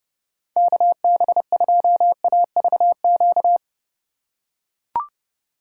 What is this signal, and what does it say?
Morse code, heard as a keyed tone, about 31 wpm, 700 Hz. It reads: KB2AVQ
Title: USA Call Signs - 6 Characters - 2x Character Spacing - 30wpm